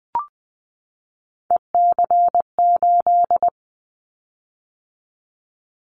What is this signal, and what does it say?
Morse code, heard as a keyed tone, about 20 wpm, 700 Hz. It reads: EC8
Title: Callsign Prefixes for Top 100 DXCC - 20wpm